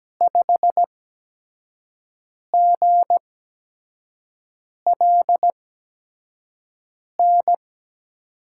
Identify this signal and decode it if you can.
Morse code, heard as a keyed tone, about 17 wpm, spaced as for 5 wpm, 700 Hz. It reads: 5GLN
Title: USA Call Signs - 5 Characters - 8x Character Spacing - 17wpm